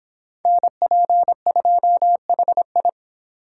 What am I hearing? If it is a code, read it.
Morse code, heard as a keyed tone, about 25 wpm, 700 Hz. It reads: NP2HI